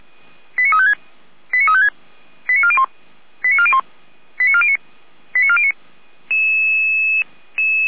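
Five-tone selective call (ZVEI), heard as quick runs of five tones, tones of about 70 ms. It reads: 89356 89356 89401 89401 89409 89409
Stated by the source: ZVEI five-tone selective call received on an SDR, recorded by feuerwehr-orlamuende (fire brigade Orlamünde).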